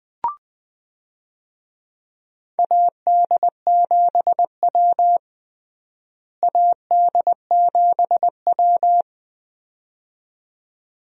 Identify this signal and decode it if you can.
Morse code, heard as a keyed tone, about 20 wpm, 700 Hz. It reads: AD7W AD7W
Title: USA Call Signs - 4 Characters - Repeated 2x - 20wpm